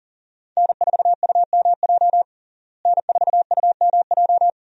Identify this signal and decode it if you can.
Morse code, heard as a keyed tone, about 40 wpm, 700 Hz. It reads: N4UMJ N4UMJ
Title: USA Call Signs - 5 Characters - Repeated 2x - 40wpm